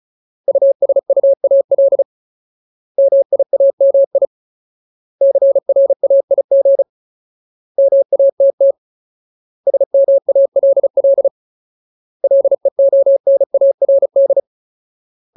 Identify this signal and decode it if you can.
Morse code, heard as a keyed tone, about 34 wpm, 550 Hz. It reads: USUAL MIAMI CRAIG MATT SMALL LEONARD
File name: Ultimate Word List / 35WPM Ultimate